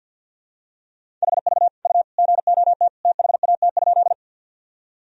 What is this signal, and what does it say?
Morse code, heard as a keyed tone, about 46 wpm, 700 Hz. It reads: HV U GOT THAT?
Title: Sentences from Top 300 Words and Abbreviations 50wpm